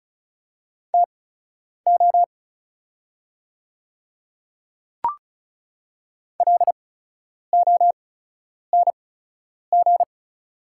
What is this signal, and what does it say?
Morse code, heard as a keyed tone, about 36 wpm, spaced as for 10 wpm, 700 Hz. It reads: TO LONG
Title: 8x Character Spacing - Top 100 Words - 35wpm